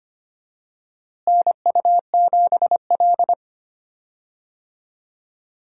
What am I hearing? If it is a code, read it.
Morse code, heard as a keyed tone, about 25 wpm, 700 Hz. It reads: NU7L
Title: USA Call Signs - 4 Characters - Repeated 3x - 25wpm